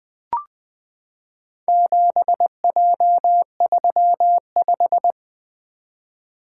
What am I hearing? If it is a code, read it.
Morse code, heard as a keyed tone, about 20 wpm, 700 Hz. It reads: 7J35
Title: Four Letter-Number Combinations - 20wpm